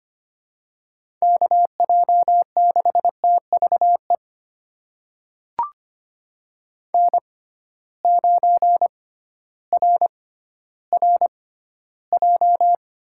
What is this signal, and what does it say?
Morse code, heard as a keyed tone, about 25 wpm, spaced as for 13 wpm, 700 Hz. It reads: KJ6TVE N9RRJ
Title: USA Call Signs - 6x Character Spacing - 25wpm